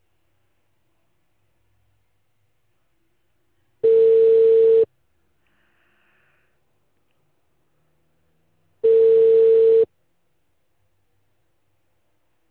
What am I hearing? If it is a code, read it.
Ringback tone (China, 450 Hz).